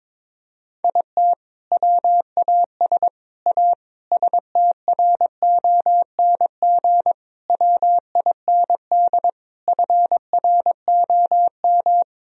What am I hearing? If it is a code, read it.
Morse code, heard as a keyed tone, about 22 wpm, 700 Hz. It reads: IT WAS A STRONG WIND FROM
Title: Sentences from Top 1000 Words - Review Part 2 - 22wpm